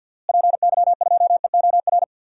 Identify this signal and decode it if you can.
Morse code, heard as a keyed tone, about 52 wpm, 700 Hz. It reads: PY2EOR
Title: DXCC-100 Call Signs - 6 Characters - Repeated 2x - 50wpm